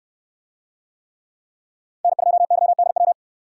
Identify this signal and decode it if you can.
Morse code, heard as a keyed tone, about 64 wpm, 700 Hz. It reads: N2YDW